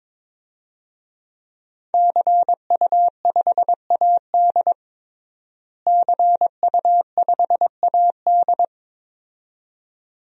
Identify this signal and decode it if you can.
Morse code, heard as a keyed tone, about 22 wpm, 700 Hz. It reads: CU5AD CU5AD
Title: DXCC-100 Call Signs - 5 Characters - Repeated 2x - 22wpm